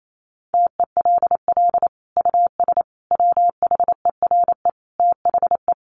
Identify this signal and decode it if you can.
Morse code, heard as a keyed tone, about 28 wpm, 700 Hz. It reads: TELL US WHERE THE
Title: Sentences from Top 1200 Words 28wpm